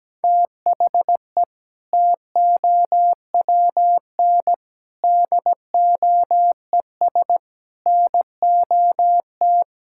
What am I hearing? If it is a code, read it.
Morse code, heard as a keyed tone, about 17 wpm, 700 Hz. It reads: THE TOWN DOES NOT
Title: Sentences from Top 1000 Words - Review Part 5 - 17wpm